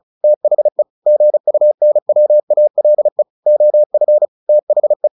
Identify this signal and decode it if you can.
Morse code, heard as a keyed tone, about 34 wpm, 600 Hz. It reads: THE GUNWALE OF THE
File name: the old man and the sea / the old man and the sea 1 35wpm